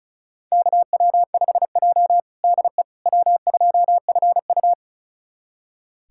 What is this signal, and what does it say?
Morse code, heard as a keyed tone, about 36 wpm, 700 Hz. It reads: KW5J DE W2FU
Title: USA Callsign DE Callsign - 4 Characters - 35wpm